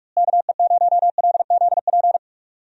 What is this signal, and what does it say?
Morse code, heard as a keyed tone, about 45 wpm, 700 Hz. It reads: KE0LZP
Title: USA Call Signs - 6 Characters - Repeated 3x - 45wpm